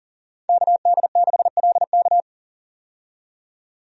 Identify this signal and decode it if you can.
Morse code, heard as a keyed tone, about 38 wpm, 700 Hz. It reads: KD6LK
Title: USA Call Signs - 5 Characters - 40wpm